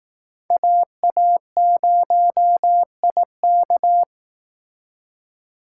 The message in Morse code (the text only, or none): AA0IK